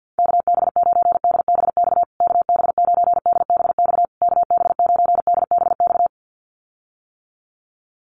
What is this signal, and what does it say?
Morse code, heard as a keyed tone, about 50 wpm, 700 Hz. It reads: KB9DBX KB9DBX KB9DBX
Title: USA Call Signs Repeated 3x 50wpm